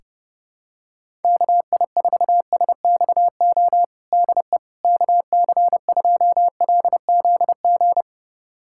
Morse code, heard as a keyed tone, about 30 wpm, 700 Hz. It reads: KI4SXO DE KC2LZG
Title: USA Callsign DE Callsign 30wpm